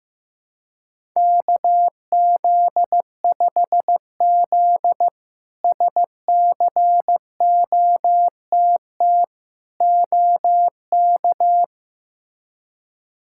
Morse code, heard as a keyed tone, about 16 wpm, 700 Hz. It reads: KZ5Z SCOTT OK
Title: K1USN SST QSO Practice - 15wpm